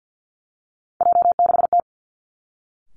Morse code, heard as a keyed tone, about 49 wpm, 700 Hz. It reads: J6T
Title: Three Challenging Letter-Number Combinations 50wpm